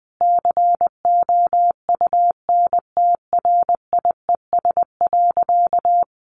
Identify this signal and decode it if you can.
Morse code, heard as a keyed tone, about 20 wpm, 700 Hz. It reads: COUNTRIES.
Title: A Sound Of Thunder 20wpm